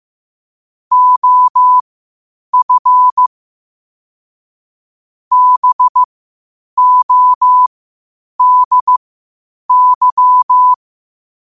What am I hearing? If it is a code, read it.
Morse code, heard as a keyed tone, about 15 wpm, spaced as for 8 wpm, 1000 Hz. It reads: OF BODY
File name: qotdfiles / 8wpm